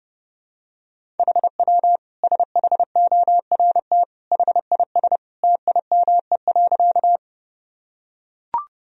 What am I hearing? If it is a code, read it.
Morse code, heard as a keyed tone, about 30 wpm, 700 Hz. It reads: HW SHORT HIS TIME.
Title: Sentences from Top 400 Words and Abbreviations 30wpm